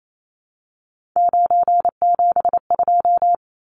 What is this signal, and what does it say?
Morse code, heard as a keyed tone, about 28 wpm, 700 Hz. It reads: 972
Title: Numbers - 3-Digit 28wpm